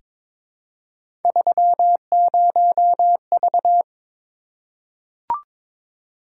Morse code, heard as a keyed tone, about 22 wpm, 700 Hz. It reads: 30V
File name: practice / 22wpm / Three Letter-Number Combinations - 22wpm